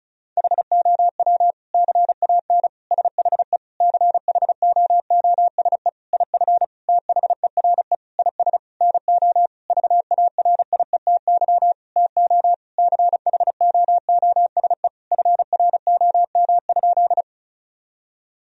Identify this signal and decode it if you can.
Morse code, heard as a keyed tone, about 36 wpm, 700 Hz. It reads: HOW CAN SHE CHOOSE IF THERE IS NO VARIETY TO CHOOSE FROM?
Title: Sentences from Top 1000 Words - Review Part 4 - 35wpm